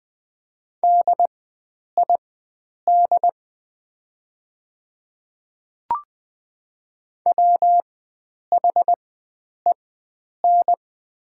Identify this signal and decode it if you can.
Morse code, heard as a keyed tone, about 20 wpm, spaced as for 9 wpm, 700 Hz. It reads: DID WHEN
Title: 4x Character Spacing - Top 500 Words - 20wpm